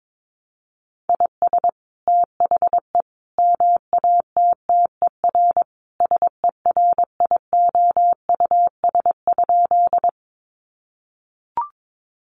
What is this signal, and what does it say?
Morse code, heard as a keyed tone, about 22 wpm, 700 Hz. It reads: IS THE MATTER SERIOUS?
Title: Sentences from Top 1100 Words 22wpm